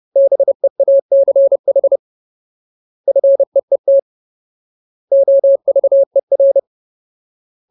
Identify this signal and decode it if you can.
Morse code, heard as a keyed tone, about 30 wpm, 550 Hz. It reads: BEACH FEET OVER
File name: Ultimate Word List / 30WPM Ultimate